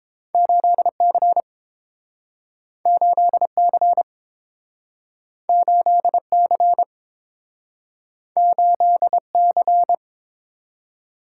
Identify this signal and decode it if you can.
Morse code, heard as a keyed tone, about 29 wpm, 700 Hz. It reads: 8C 8C 8C 8C